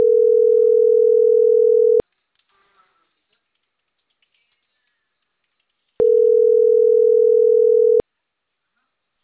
Ringback tone (North America), 440 + 480 Hz.